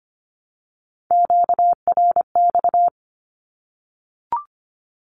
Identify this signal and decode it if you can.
Morse code, heard as a keyed tone, about 25 wpm, 700 Hz. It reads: QRX
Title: Q-Codes Repeated 3x 25wpm